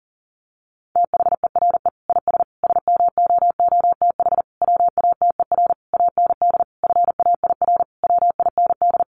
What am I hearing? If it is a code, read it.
Morse code, heard as a keyed tone, about 40 wpm, 700 Hz. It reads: THERE IS SMOOTH WATER AND FAIR WIND
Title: Sentences from Top 1400 Words 40wpm